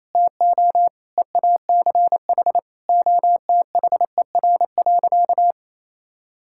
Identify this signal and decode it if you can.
Morse code, heard as a keyed tone, about 28 wpm, 700 Hz. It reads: TO EACH OTHER.
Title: Sentences from Top 1000 Words - Review Part 5 - 28wpm